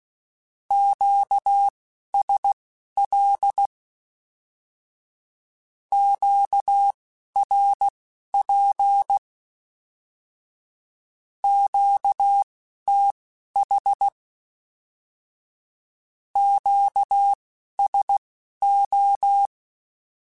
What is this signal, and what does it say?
Morse code, heard as a keyed tone, about 16 wpm, spaced as for 11 wpm, 800 Hz. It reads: QSL QRP QTH QSO
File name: Q Calls / Q Calls-mp3s / Q Calls Once @10